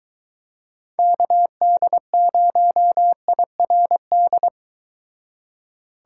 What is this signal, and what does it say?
Morse code, heard as a keyed tone, about 23 wpm, 700 Hz. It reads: KD0IRD